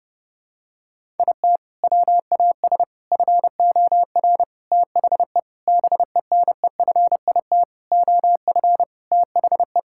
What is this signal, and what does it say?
Morse code, heard as a keyed tone, about 30 wpm, 700 Hz. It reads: IT WAS FOR THE BENEFIT OF THE
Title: Sentences from Top 1000 Words - Review Part 6 - 30wpm